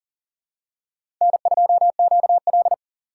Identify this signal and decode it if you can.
Morse code, heard as a keyed tone, about 40 wpm, 700 Hz. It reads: N2QL
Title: USA Call Signs - 4 Characters - 2x Character Spacing - 40wpm